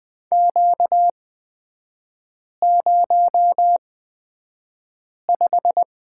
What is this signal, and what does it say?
Morse code, heard as a keyed tone, about 20 wpm, spaced as for 5 wpm, 700 Hz. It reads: Q05
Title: 236 Lesson 38e - Letter q - Three Characters 20wpm